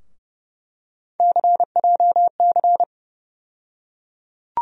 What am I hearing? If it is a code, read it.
Morse code, heard as a keyed tone, about 30 wpm, 700 Hz. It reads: CJC
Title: USA Call Sign Postfixes - 3 Letters 30wpm